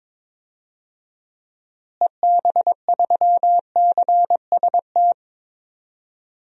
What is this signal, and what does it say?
Morse code, heard as a keyed tone, about 22 wpm, 700 Hz. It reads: EB3CST
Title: DXCC-100 Call Signs - 6 Characters - Repeated 2x - 22wpm